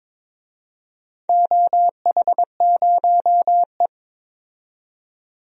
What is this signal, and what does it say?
Morse code, heard as a keyed tone, about 21 wpm, 700 Hz. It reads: OH0E
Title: DXCC-100 Call Signs - 4 Characters - 22wpm